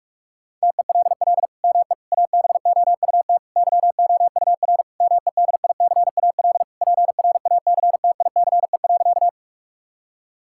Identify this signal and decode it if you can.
Morse code, heard as a keyed tone, about 46 wpm, 700 Hz. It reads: TELL ME ABOUT YOUR MEDICAL PRACTICE.
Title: Sentences from Top 1000 Words - Review Part 5 - 45wpm